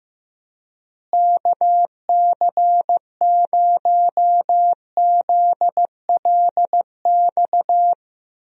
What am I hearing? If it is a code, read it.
Morse code, heard as a keyed tone, about 15 wpm, 700 Hz. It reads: KC0ZLX
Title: USA Call Signs - 6 Characters - Repeated 2x - 15wpm